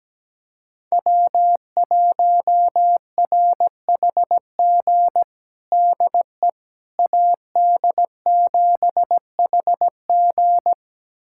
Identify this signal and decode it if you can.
Morse code, heard as a keyed tone, about 17 wpm, 700 Hz. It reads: W1RHG DE AD7HG
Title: USA Callsign DE Callsign - 5 Characters - 17wpm